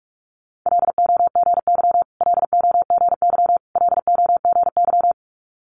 Morse code, heard as a keyed tone, about 46 wpm, 690 Hz. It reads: LOGY LOGY LOGY
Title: Multi-letter Phonemes Repeated 3x 45wpm